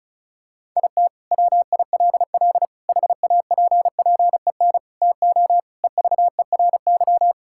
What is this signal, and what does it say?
Morse code, heard as a keyed tone, about 36 wpm, 700 Hz. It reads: IT WILL HAPPEN TO EVERY
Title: Sentences from Top 1000 Words - Review Part 6 - 35wpm